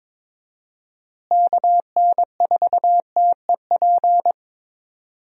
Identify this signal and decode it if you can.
Morse code, heard as a keyed tone, about 22 wpm, 700 Hz. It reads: KN4TEP
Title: USA Call Signs - 6 Characters - Repeated 3x - 22wpm